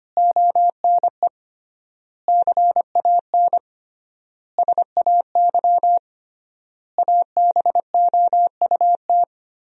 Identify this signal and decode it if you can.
Morse code, heard as a keyed tone, about 25 wpm, 700 Hz. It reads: ONE CAN SAY ABOUT
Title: 3x Word-Spacing - Sentences from Top 200 Words - 25wpm